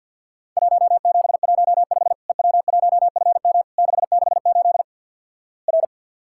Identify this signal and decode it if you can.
Morse code, heard as a keyed tone, about 50 wpm, 700 Hz. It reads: 1715 EW1FM 668 R
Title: Worked All Europe DX Contest - QTC Exchange with Repeats - 50wpm